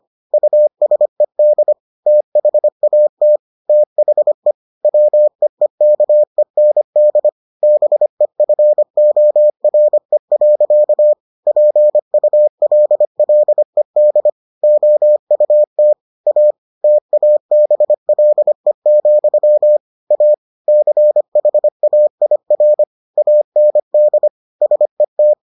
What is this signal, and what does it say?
Morse code, heard as a keyed tone, about 25 wpm, 600 Hz. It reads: USED THAT THE WEEKEND BEFORE. PULLED OUT A TABLE, A CHAIR AND SET